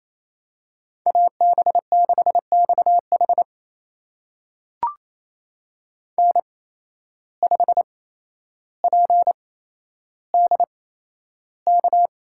Morse code, heard as a keyed tone, about 28 wpm, spaced as for 12 wpm, 700 Hz. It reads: AB6XH N5PDK
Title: USA Call Signs - 5 Characters - 8x Character Spacing - 28wpm